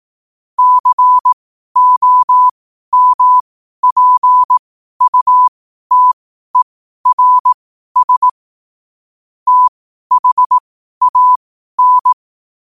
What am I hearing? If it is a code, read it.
Morse code, heard as a keyed tone, about 18 wpm, spaced as for 12 wpm, 1000 Hz. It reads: COMPUTERS THAN
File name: qotdfiles / 12wpm